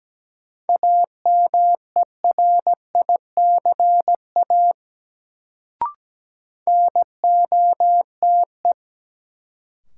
Morse code, heard as a keyed tone, about 17 wpm, 700 Hz. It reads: AMERICA NOTE
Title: Sets of 5 Words - New Word Drill - 17wpm